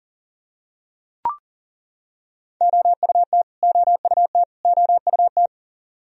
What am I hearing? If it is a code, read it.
Morse code, heard as a keyed tone, about 40 wpm, 700 Hz. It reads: OUT OUT OUT
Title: Top 100 Words Repeated 40wpm